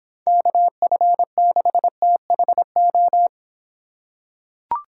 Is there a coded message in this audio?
Yes: Morse code from 0.3 to 3.3 s, heard as a keyed tone, about 26 wpm, 700 Hz. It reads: KF6THO